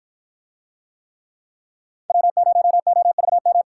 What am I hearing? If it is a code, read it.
Morse code, heard as a keyed tone, about 53 wpm, 700 Hz. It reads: W0OVM